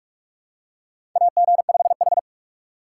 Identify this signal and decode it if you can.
Morse code, heard as a keyed tone, about 44 wpm, 700 Hz. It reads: AG5H
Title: USA Call Signs - 4 Characters - 2x Character Spacing - 45wpm